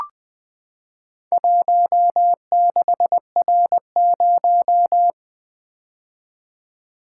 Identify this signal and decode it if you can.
Morse code, heard as a keyed tone, about 20 wpm, 700 Hz. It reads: 16R0
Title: Four Letter-Number Combinations - 20wpm